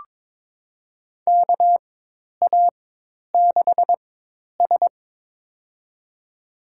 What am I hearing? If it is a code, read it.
Morse code, heard as a keyed tone, about 21 wpm, spaced as for 10 wpm, 700 Hz. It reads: KA6S
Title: USA Call Signs - 4 Characters - 4x Character Spacing - 22wpm